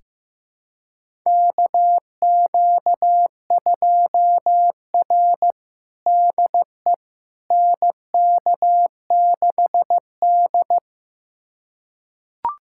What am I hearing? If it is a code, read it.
Morse code, heard as a keyed tone, about 15 wpm, 700 Hz. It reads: KQ2R DE NK6D
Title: USA Callsign DE Callsign - 4 Characters - 15wpm